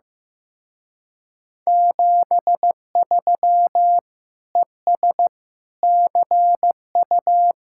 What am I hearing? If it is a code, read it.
Morse code, heard as a keyed tone, about 15 wpm, 700 Hz. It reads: 73 ES CU
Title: Speed-Racing Common QSO Phrases Encore - 15wpm